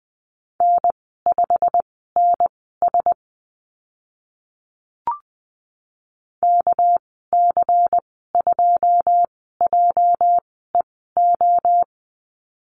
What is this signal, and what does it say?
Morse code, heard as a keyed tone, about 20 wpm, spaced as for 15 wpm, 700 Hz. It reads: N5NS KC2JEO
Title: 207 Lesson 33g - Letter j - Call Signs 20wpm